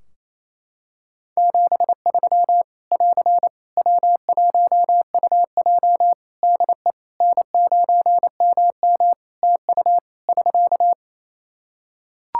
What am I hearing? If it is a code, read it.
Morse code, heard as a keyed tone, about 28 wpm, 700 Hz. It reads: 73 <AR> W1UJ DE N9MM TU <SK>
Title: Practice QSO - Exchange 4 - Ending - 28wpm